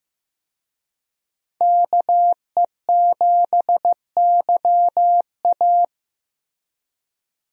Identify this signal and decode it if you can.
Morse code, heard as a keyed tone, about 15 wpm, 700 Hz. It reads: KE7YA
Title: USA Call Signs - 5 Characters - Repeated 3x - 15wpm